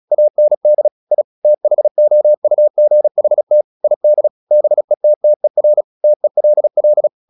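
Morse code, heard as a keyed tone, about 36 wpm, 600 Hz. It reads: AND I THOUGHT ID BETTER TELL
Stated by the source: Morse practice at 36 wpm, text of Ken Follett's novel Never, from CW med Gnister.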